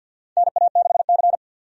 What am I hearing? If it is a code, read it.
Morse code, heard as a keyed tone, about 51 wpm, 700 Hz. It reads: NA6C
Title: USA Call Signs - 4 Characters - 50wpm